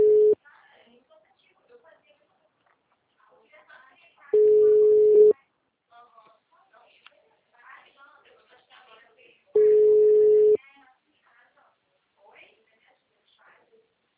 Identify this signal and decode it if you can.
Ringback tone (Europe, 425 Hz), with speech.